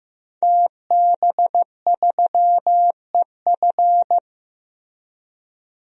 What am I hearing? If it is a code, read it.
Morse code, heard as a keyed tone, about 15 wpm, 700 Hz. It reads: TB3EF
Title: DXCC-100 Call Signs - 5 Characters - Repeated 3x - 15wpm